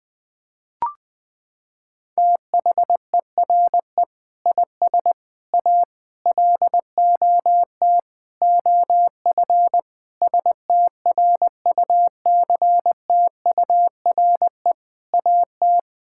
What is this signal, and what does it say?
Morse code, heard as a keyed tone, about 20 wpm, 700 Hz. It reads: THERE IS A LOT OF STRUCTURE AT
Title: Sentences from Top 1000 Words - Review Part 6 - 20wpm